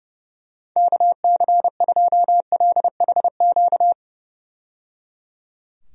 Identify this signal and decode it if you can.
Morse code, heard as a keyed tone, about 30 wpm, 700 Hz. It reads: KC2LHQ
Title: USA Call Signs - 6 Characters - 30wpm